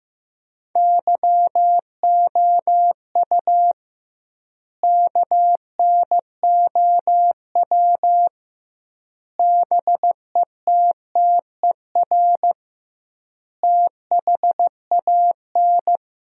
Morse code, heard as a keyed tone, about 15 wpm, 700 Hz. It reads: YOU KNOW BETTER THAN
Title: 2x Word-Spacing - Sentences from Top 400 Words - 15wpm